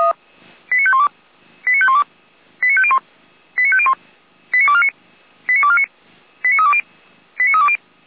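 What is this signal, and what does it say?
Five-tone selective call (ZVEI), heard as quick runs of five tones, tones of about 70 ms. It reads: 89512 89512 89501 89501 89249 89249 89320 89320
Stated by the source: ZVEI five-tone call 89229 received on SDR.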